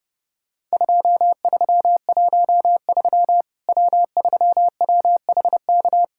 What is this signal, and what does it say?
Morse code, heard as a keyed tone, about 30 wpm, 700 Hz. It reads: 2313 W3WHK
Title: Worked All Europe DX Contest - QTC Exchange - 30wpm